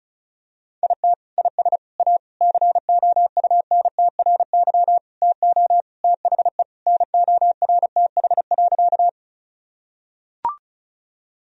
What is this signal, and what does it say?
Morse code, heard as a keyed tone, about 36 wpm, 700 Hz. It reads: IT IS A COUNTRY TO THE NORTH.
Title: Sentences from Top 1000 Words - Review Part 1 - 35wpm